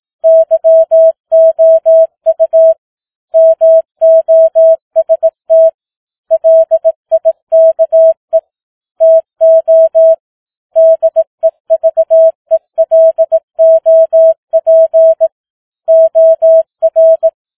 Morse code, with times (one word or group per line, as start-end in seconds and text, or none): 0.2-2.7 YOU
3.4-5.7 MOST
6.3-8.4 LIKE
9.0-10.1 TO
10.8-15.3 DEVELOP
15.9-17.3 OR